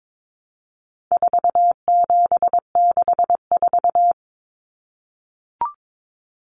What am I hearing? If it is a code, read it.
Morse code, heard as a keyed tone, about 22 wpm, 700 Hz. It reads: 4764
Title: Numbers - 4-Digit 22wpm